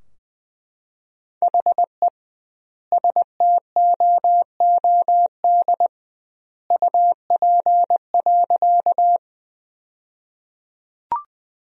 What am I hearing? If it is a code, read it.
Morse code, heard as a keyed tone, about 21 wpm, 700 Hz. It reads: HE STOOD UP.
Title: 2x Word-Spacing - Sentences from Top 500 Words - 20wpm